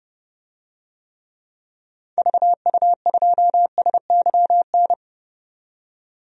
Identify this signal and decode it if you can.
Morse code, heard as a keyed tone, about 32 wpm, 700 Hz. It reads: VU2SYN
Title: DXCC-100 Call Signs - 6 Characters - 30wpm